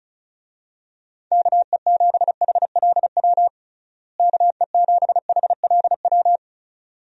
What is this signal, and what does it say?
Morse code, heard as a keyed tone, about 36 wpm, 700 Hz. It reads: KE7HLW KE7HLW
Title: USA Call Signs - 6 Characters - Repeated 2x - 35wpm